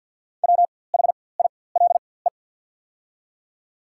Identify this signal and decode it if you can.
Morse code, heard as a keyed tone, about 54 wpm, spaced as for 24 wpm, 700 Hz. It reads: WHILE
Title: 4x Character Spacing - Top 300 Words - 50wpm